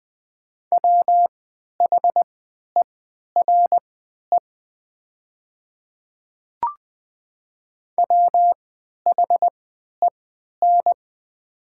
Morse code, heard as a keyed tone, about 20 wpm, spaced as for 11 wpm, 700 Hz. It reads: WHERE WHEN
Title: 3x Character Spacing - Top 200 Words - 20wpm